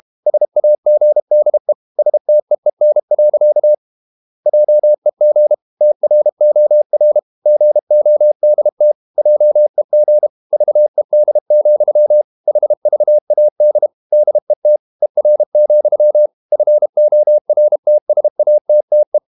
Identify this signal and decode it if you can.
Morse code, heard as a keyed tone, about 32 wpm, 600 Hz. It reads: SAGDE STEEN. JEG TROR GODT JEG VED, HVAD DET ER, FORTSATTE
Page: Ferie 32 wpm